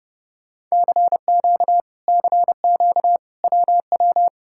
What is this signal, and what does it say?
Morse code, heard as a keyed tone, about 34 wpm, 700 Hz. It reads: CQ CQ WW